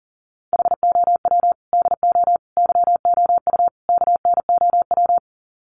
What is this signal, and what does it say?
Morse code, heard as a keyed tone, about 40 wpm, 700 Hz. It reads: HOW DO YOU KNOW
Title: Sets of 4 Words 40wpm